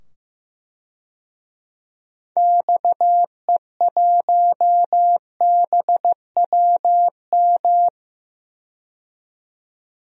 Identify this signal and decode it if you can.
Morse code, heard as a keyed tone, about 15 wpm, 700 Hz. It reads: XE1BWM
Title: DXCC-100 Call Signs - 6 Characters - 15wpm